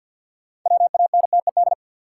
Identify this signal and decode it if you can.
Morse code, heard as a keyed tone, about 52 wpm, 700 Hz. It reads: WANTED
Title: Sets of 2 Words - New Word Drill - 50wpm